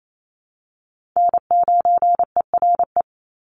Morse code, heard as a keyed tone, about 28 wpm, 700 Hz. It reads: N9ERE
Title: USA Call Signs 28wpm